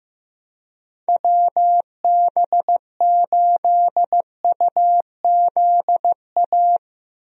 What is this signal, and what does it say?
Morse code, heard as a keyed tone, about 15 wpm, 700 Hz. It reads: WB8UZA
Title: USA Call Signs - 6 Characters - Repeated 2x - 15wpm